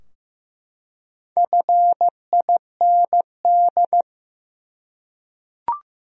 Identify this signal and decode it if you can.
Morse code, heard as a keyed tone, about 15 wpm, 700 Hz. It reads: FIND
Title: Top 100 Words 15wpm